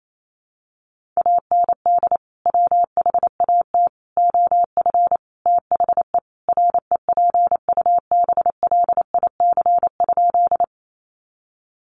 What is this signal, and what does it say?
Morse code, heard as a keyed tone, about 28 wpm, 700 Hz. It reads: AND WHAT OF THE REPUBLIC?
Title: Sentences from Top 1500 Words 28wpm